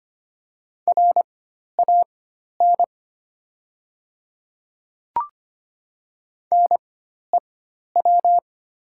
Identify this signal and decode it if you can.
Morse code, heard as a keyed tone, about 25 wpm, spaced as for 12 wpm, 690 Hz. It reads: RAN NEW
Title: 4x Character Spacing - Top 500 Words - 25wpm